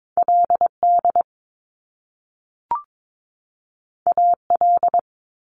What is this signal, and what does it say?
Morse code, heard as a keyed tone, about 22 wpm, 700 Hz. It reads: LD AL
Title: Top Two Letter Combinations in Words 22wpm